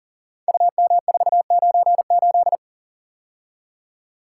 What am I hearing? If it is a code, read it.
Morse code, heard as a keyed tone, about 41 wpm, 690 Hz. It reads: UM498
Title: Code Groups - 5 Letter:Numbers - 40wpm